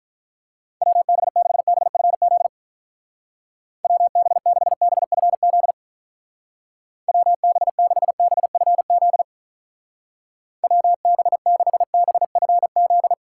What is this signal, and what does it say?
Morse code, heard as a keyed tone, about 45 wpm, 700 Hz. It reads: WB6BFZ WB6BFZ WB6BFZ WB6BFZ